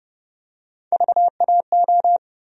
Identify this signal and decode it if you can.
Morse code, heard as a keyed tone, about 30 wpm, 700 Hz. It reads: VAO
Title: USA Call Sign Postfixes - 3 Letters 30wpm